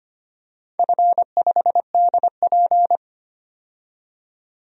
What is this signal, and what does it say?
Morse code, heard as a keyed tone, about 25 wpm, 700 Hz. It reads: F5DP
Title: DXCC-100 Call Signs - 4 Characters - 25wpm